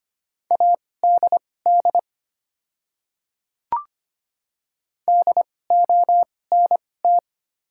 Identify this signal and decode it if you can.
Morse code, heard as a keyed tone, about 25 wpm, spaced as for 18 wpm, 700 Hz. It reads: ADD DONT
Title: 2x Character Spacing - Top 300 Words - 25wpm